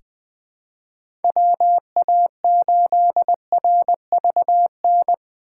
Morse code, heard as a keyed tone, about 20 wpm, 700 Hz. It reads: WA8RVN
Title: USA Call Signs - 6 Characters - 3x Character Spacing - 20wpm